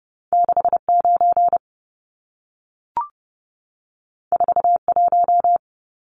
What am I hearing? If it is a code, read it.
Morse code, heard as a keyed tone, about 30 wpm, 700 Hz. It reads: 69 41
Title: Numbers - 2-Digit 30wpm